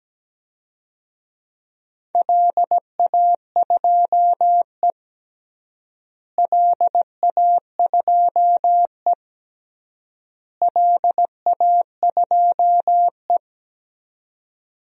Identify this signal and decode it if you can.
Morse code, heard as a keyed tone, about 17 wpm, 700 Hz. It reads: LA2E LA2E LA2E
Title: DXCC-100 Call Signs - 4 Characters - Repeated 3x - 17wpm